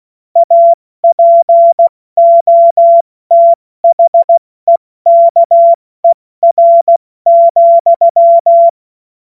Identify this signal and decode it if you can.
Morse code, heard as a keyed tone, about 16 wpm, 680 Hz. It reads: APOTHEKER,
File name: Morsekurs / Witze / Witz 02 Januar 2024 70 F80